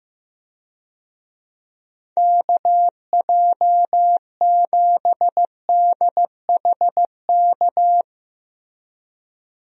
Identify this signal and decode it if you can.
Morse code, heard as a keyed tone, about 15 wpm, 700 Hz. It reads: KJ7DHK
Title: USA Call Signs - 6 Characters - Repeated 2x - 15wpm